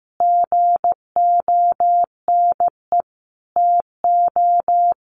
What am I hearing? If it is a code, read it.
Morse code, heard as a keyed tone, about 15 wpm, 700 Hz. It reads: GONE TO
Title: A Sound Of Thunder 15wpm